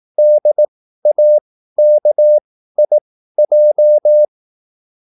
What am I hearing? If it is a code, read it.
Morse code, heard as a keyed tone, about 18 wpm, spaced as for 13 wpm, 600 Hz. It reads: DAKIJ